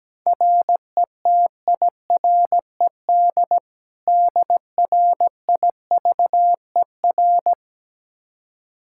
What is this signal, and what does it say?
Morse code, heard as a keyed tone, about 17 wpm, 700 Hz. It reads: RETIRED DRIVER
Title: Practice QSO - Exchange 3 - More Info - 17wpm